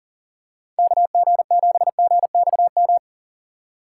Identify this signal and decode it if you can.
Morse code, heard as a keyed tone, about 40 wpm, 700 Hz. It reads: KG7GXM
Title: USA Call Signs - 6 Characters - 3x Character Spacing - 40wpm